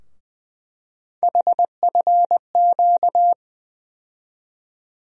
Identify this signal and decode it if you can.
Morse code, heard as a keyed tone, about 20 wpm, 700 Hz. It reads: HFQ
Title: USA Call Sign Postfixes - 3 Letters 20wpm